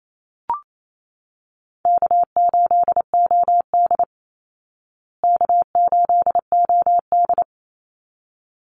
Morse code, heard as a keyed tone, about 28 wpm, 700 Hz. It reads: K8OD K8OD
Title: Call Signs Repeated 3x - 28wpm